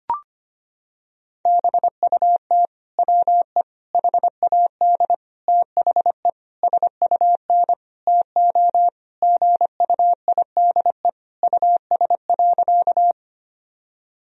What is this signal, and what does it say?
Morse code, heard as a keyed tone, about 25 wpm, 700 Hz. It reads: BUT WE HAD THE SUN TO GUIDE US.
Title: Sentences from Top 1000 Words - Review Part 3 - 25wpm